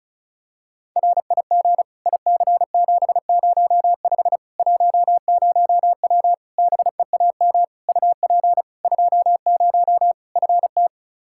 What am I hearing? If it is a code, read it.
Morse code, heard as a keyed tone, about 36 wpm, 700 Hz. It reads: RIG IC705 10W BEAM UP 20 FT
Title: Practice QSO - Exchange 2 - Info - 35wpm